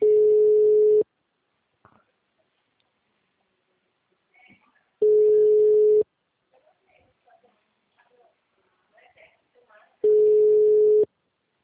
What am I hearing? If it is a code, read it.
Ringback tone (Europe, 425 Hz), with speech.